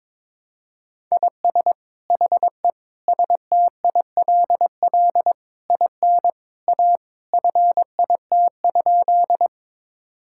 Morse code, heard as a keyed tone, about 22 wpm, 700 Hz. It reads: IS HE STILL IN A FIT?
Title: Sentences from Top 1000 Words - Review Part 5 - 22wpm